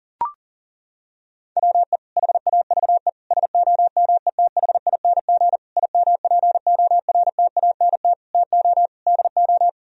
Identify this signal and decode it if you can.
Morse code, heard as a keyed tone, about 37 wpm, 700 Hz. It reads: WE HAVE SOMETHING IMPORTANT TO DO
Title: Sentences from Top 1000 Words - Review Part 4 - 40wpm